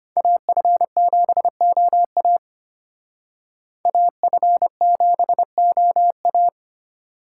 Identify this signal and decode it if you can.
Morse code, heard as a keyed tone, about 29 wpm, 700 Hz. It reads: AF7OA AF7OA
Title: Speed Racing - USA Call Signs 25wpm